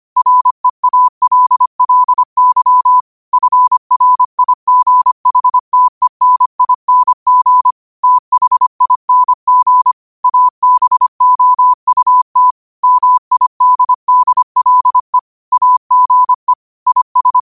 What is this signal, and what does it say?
Morse code, heard as a keyed tone, about 25 wpm, 1000 Hz. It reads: REALLY FRIGHTENING THING ABOUT MIDDLE AGE IS
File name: qotdfiles / 25wpm